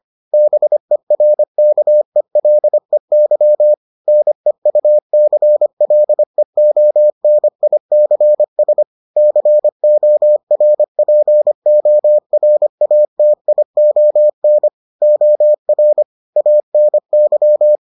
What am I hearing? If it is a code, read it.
Morse code, heard as a keyed tone, about 25 wpm, 600 Hz. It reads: BERKELEY NEUCLEONICS CORPORATION OR ANY